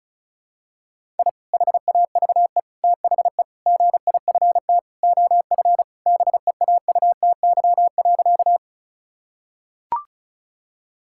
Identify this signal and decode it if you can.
Morse code, heard as a keyed tone, about 36 wpm, 700 Hz. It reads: I HAVE THE GIFT OF BEAUTY.
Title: Sentences from Top 1000 Words - Review Part 6 - 35wpm